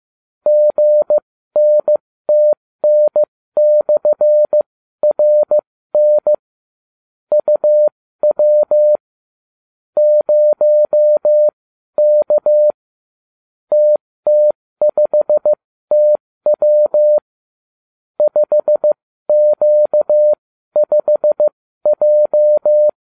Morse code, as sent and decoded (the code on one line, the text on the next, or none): --. -. - -. -..-. .-. -. / ..- .-- / ----- -.- / - - ..... - .-- / ..... --.- ..... .---
GNTN/RN UW 0K TT5TW 5Q5J